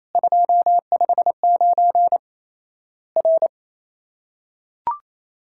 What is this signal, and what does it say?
Morse code, heard as a keyed tone, about 28 wpm, 700 Hz. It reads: 259 R
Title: Worked All Europe DX Contest - QTC Exchange - 28wpm